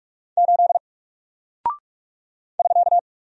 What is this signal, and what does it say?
Morse code, heard as a keyed tone, about 44 wpm, 700 Hz. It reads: 8 <SK>
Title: Single Character (Letter:Number:Punctuation:Prosign) Alt 45wpm